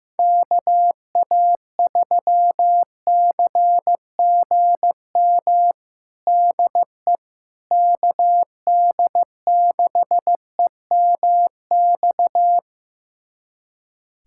Morse code, heard as a keyed tone, about 16 wpm, 700 Hz. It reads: KA3CGM DE KD6EMX
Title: USA Callsign DE Callsign 15wpm